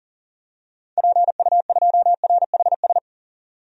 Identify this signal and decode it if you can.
Morse code, heard as a keyed tone, about 40 wpm, 700 Hz. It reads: PU2RHS
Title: DXCC-100 Call Signs - 6 Characters - 40wpm